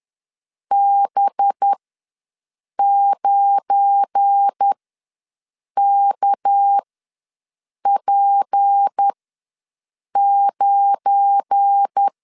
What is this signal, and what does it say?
Morse code, heard as a keyed tone, about 10 wpm, spaced as for 6 wpm, 800 Hz. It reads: B9KP9
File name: LEZIONI / 7 GF9Z / 7 GF9 ws 10wpm 4034